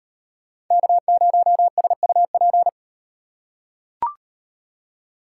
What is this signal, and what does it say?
Morse code, heard as a keyed tone, about 38 wpm, 700 Hz. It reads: K0SUP